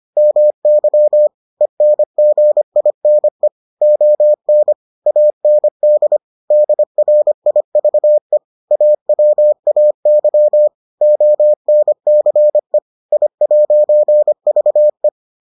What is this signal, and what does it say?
Morse code, heard as a keyed tone, about 25 wpm, 600 Hz. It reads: MY ENGINE ON AND DRIVE AWAY ONCE I'VE